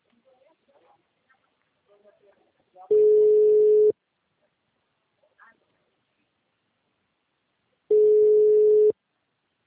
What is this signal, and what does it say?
Ringback tone (Europe, 425 Hz), with speech.